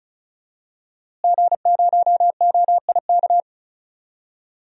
Morse code, heard as a keyed tone, about 34 wpm, 700 Hz. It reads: G0OIK
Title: DXCC-100 Call Signs - 5 Characters - Repeated 2x - 35wpm